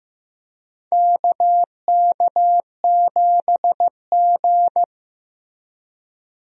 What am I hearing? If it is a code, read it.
Morse code, heard as a keyed tone, about 15 wpm, 700 Hz. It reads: KK7G
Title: USA Call Signs - 4 Characters - Repeated 3x - 15wpm